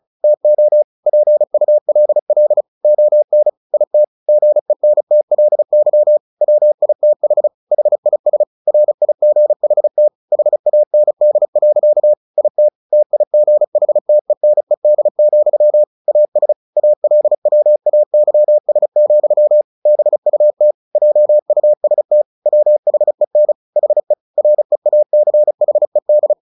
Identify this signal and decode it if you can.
Morse code, heard as a keyed tone, about 34 wpm, 600 Hz. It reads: TO PULL ON IT GENTLY WITH HIS RIGHT HAND. IT TIGHTENED, AS ALWAYS, BUT JUST WHEN HE REACHED
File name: the old man and the sea 2 35wpm